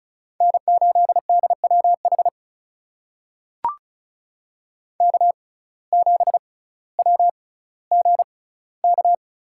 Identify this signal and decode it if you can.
Morse code, heard as a keyed tone, about 34 wpm, spaced as for 18 wpm, 700 Hz. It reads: N8DWH K7WGK
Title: USA Call Signs - 5 Characters - 6x Character Spacing - 35wpm